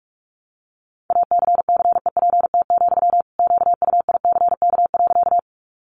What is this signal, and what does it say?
Morse code, heard as a keyed tone, about 44 wpm, 700 Hz. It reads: ACCEPT, QUICK.
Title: Sentences from Top 1300 Words 45wpm